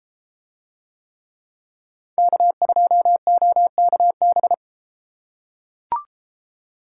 Morse code, heard as a keyed tone, about 33 wpm, 700 Hz. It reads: K2OKB